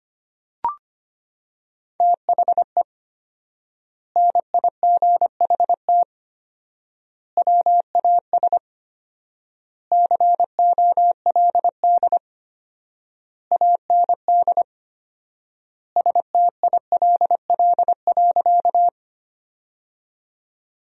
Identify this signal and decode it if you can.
Morse code, heard as a keyed tone, about 25 wpm, 700 Hz. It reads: THE NIGHT WAS COLD AND STILL.
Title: 4x Word-Spacing - Sentences from Top 400 Words - 25wpm